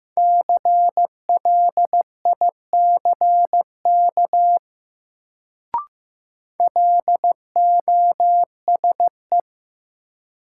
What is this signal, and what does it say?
Morse code, heard as a keyed tone, about 16 wpm, 700 Hz. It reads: CLICK LOSE
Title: Sentences from Top 1100 Words - New Word Drill - 15wpm